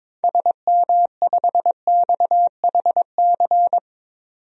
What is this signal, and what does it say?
Morse code, heard as a keyed tone, about 22 wpm, 700 Hz. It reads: SM5XHC
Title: DXCC-100 Call Signs - 6 Characters - 22wpm